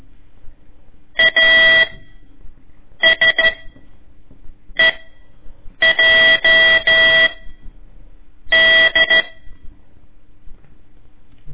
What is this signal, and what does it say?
Morse code, heard as a keyed tone, about 13 wpm, 1950 Hz. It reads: A S E J D
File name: ik1weg2 / CORSI CW / lez4-1